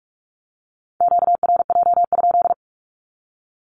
Morse code, heard as a keyed tone, about 44 wpm, 700 Hz. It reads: QRJ?
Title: Q-Codes 45wpm